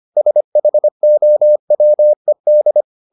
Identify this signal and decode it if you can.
Morse code, heard as a keyed tone, about 25 wpm, 600 Hz. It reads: SHOWED